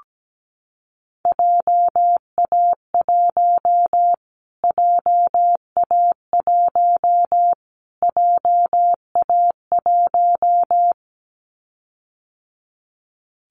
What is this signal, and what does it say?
Morse code, heard as a keyed tone, about 17 wpm, 700 Hz. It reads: JA1 JA1 JA1
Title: Callsign Prefixes for Top 25 DXCC - Repeated 3x - 17wpm